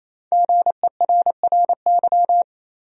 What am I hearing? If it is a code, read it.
Morse code, heard as a keyed tone, about 28 wpm, 700 Hz. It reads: GERRY
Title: USA Common Names - 28wpm